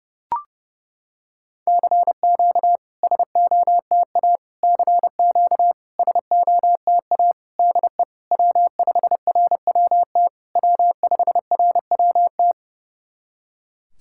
Morse code, heard as a keyed tone, about 30 wpm, 700 Hz. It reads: CQ SOTA CQ SOTA DE W5RWT W5RWT
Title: SOTA QSO Practice 30wpm